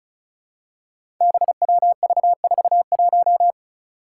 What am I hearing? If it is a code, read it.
Morse code, heard as a keyed tone, about 34 wpm, 700 Hz. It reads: BWV41
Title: Five Challenging Letter-Number Combinations 35wpm